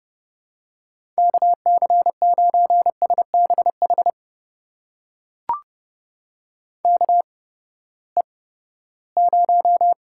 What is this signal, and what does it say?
Morse code, heard as a keyed tone, about 30 wpm, spaced as for 17 wpm, 700 Hz. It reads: KC9SBH KE0
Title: USA Call Signs - 6 Characters - 8x Character Spacing - 30wpm